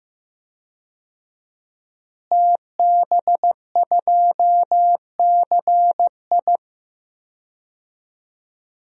Morse code, heard as a keyed tone, about 15 wpm, 700 Hz. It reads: TB2CI
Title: DXCC-100 Call Signs - 5 Characters - Repeated 2x - 15wpm